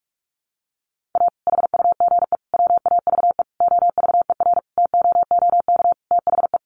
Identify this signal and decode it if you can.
Morse code, heard as a keyed tone, about 44 wpm, 700 Hz. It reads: A HUGE WAVE OVERTOOK THE
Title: Sentences from Top 1400 Words 45wpm